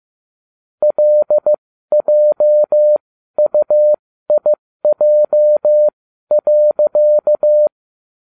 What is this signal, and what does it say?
Morse code, heard as a keyed tone, about 15 wpm, spaced as for 12 wpm, 600 Hz. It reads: LJUIJ.
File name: cw / ubacw20 2